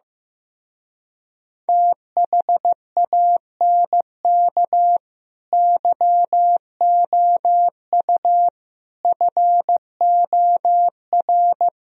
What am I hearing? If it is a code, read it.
Morse code, heard as a keyed tone, about 15 wpm, 700 Hz. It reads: THANK YOU FOR
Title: Speed-Racing - Sets of 5 Words 15wpm